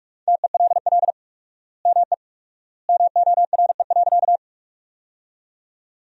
Morse code, heard as a keyed tone, about 46 wpm, 700 Hz. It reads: TELL ME MORE.
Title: 4x Word-Spacing - Sentences from Top 300 Words - 45wpm